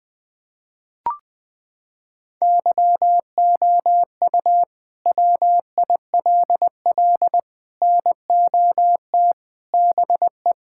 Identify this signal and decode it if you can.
Morse code, heard as a keyed tone, about 20 wpm, 700 Hz. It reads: YOU WILL NOT BE
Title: Sets of 6 Words - 20wpm